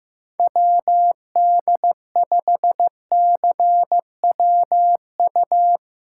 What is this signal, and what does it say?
Morse code, heard as a keyed tone, about 15 wpm, 700 Hz. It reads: WD5CWU
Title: USA Callsign DE Callsign 15wpm